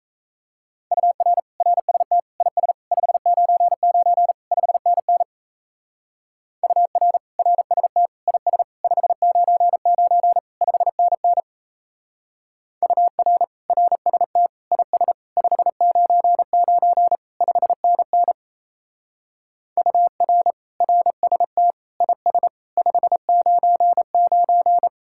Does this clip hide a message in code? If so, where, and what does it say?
Yes: Morse code from 0.9 to 24.9 s, heard as a keyed tone, about 36 wpm, 700 Hz. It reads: UR RST IS 599 5NN UR RST IS 599 5NN UR RST IS 599 5NN UR RST IS 599